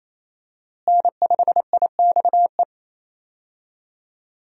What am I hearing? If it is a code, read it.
Morse code, heard as a keyed tone, about 28 wpm, 700 Hz. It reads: N5IXE
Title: USA Call Signs - 5 Characters - Repeated 3x - 28wpm